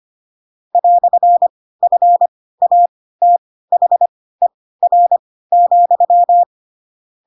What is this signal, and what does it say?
Morse code, heard as a keyed tone, about 25 wpm, spaced as for 16 wpm, 700 Hz. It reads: "FATHER,
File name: Pinocchio / Pinocchio ch7 10 25 ik1hge